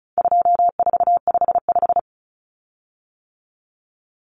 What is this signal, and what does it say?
Morse code, heard as a keyed tone, about 34 wpm, 700 Hz. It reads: 2455
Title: Common Cut Numbers - 4-Digit 35wpm